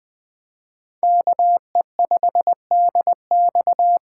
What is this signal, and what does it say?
Morse code, heard as a keyed tone, about 21 wpm, 700 Hz. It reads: KE5DX